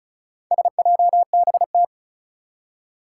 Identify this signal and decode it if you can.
Morse code, heard as a keyed tone, about 36 wpm, 700 Hz. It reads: SJBT
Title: Four Challenging Letter-Number Combinations 35wpm